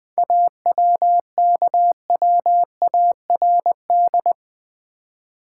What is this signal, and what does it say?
Morse code, heard as a keyed tone, about 20 wpm, 690 Hz. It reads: AWKWARD
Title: 7 Letter Words - First 3 Letters Given - 20wpm